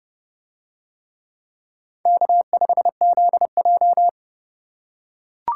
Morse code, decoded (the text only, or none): K5ZJ